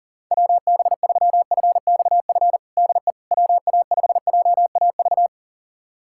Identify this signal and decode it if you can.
Morse code, heard as a keyed tone, about 37 wpm, 700 Hz. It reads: WB3FXF DE WA5JAV
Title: USA Callsign DE Callsign 40wpm